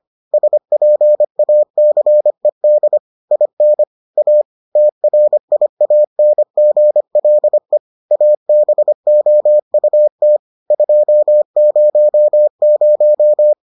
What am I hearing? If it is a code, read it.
Morse code, heard as a keyed tone, about 25 wpm, 600 Hz. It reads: SPACED IN A TRIANGLE ABOUT 200